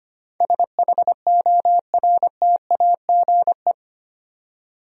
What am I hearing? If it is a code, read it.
Morse code, heard as a keyed tone, about 25 wpm, 690 Hz. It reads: SHORTAGE
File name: 8 Letter Words - First 3 Letters Given - 25wpm